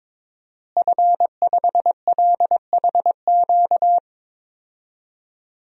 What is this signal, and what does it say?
Morse code, heard as a keyed tone, about 22 wpm, 700 Hz. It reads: F5LHQ
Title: DXCC-100 Call Signs - 5 Characters - Repeated 3x - 22wpm